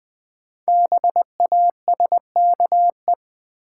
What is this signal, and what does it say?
Morse code, heard as a keyed tone, about 20 wpm, 700 Hz. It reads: BASKE
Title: Build A Word - 5 Letters At A Time - 10-Letter Words - 20wpm